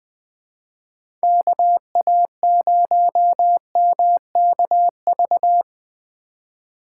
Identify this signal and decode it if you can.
Morse code, heard as a keyed tone, about 20 wpm, 700 Hz. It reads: KA0MKV
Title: USA Call Signs - 6 Characters - 4x Character Spacing - 20wpm